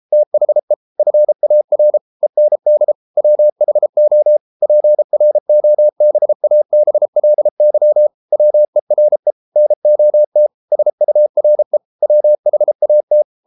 Morse code, heard as a keyed tone, about 33 wpm, 600 Hz. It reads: THE FAR END WHO PROBABLY WERE NOT SURE WHAT